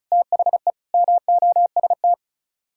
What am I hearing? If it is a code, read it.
Morse code, heard as a keyed tone, about 45 wpm, 700 Hz. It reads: THE MOST